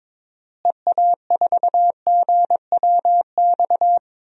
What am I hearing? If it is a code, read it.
Morse code, heard as a keyed tone, about 21 wpm, 700 Hz. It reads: EA4GWX
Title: DXCC-100 Call Signs - 6 Characters - Repeated 2x - 22wpm